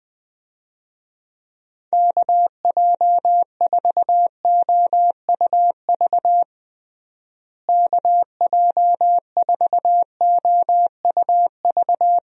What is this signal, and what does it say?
Morse code, heard as a keyed tone, about 20 wpm, 700 Hz. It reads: KJ4OUV KJ4OUV
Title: USA Call Signs - 6 Characters - Repeated 3x - 20wpm